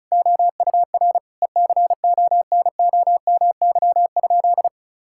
Morse code, heard as a keyed tone, about 36 wpm, 700 Hz. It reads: OUR ECONOMY?
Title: Sentences from Top 1000 Words - Review Part 5 - 35wpm